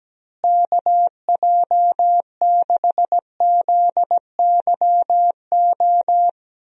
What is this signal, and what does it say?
Morse code, heard as a keyed tone, about 17 wpm, 700 Hz. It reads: KJ6ZYO
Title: USA Call Signs - 6 Characters - Repeated 3x - 17wpm